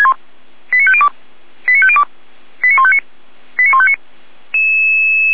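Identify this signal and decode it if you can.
Five-tone selective call (ZVEI), heard as quick runs of five tones, tones of about 70 ms. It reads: 89502 89502 89159 89159
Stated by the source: ZVEI five-tone selective call received on an SDR.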